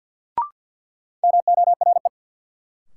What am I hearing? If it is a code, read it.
Morse code, heard as a keyed tone, about 48 wpm, 700 Hz. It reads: MORE
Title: Sets of 2 Words - New Word Drill - 50wpm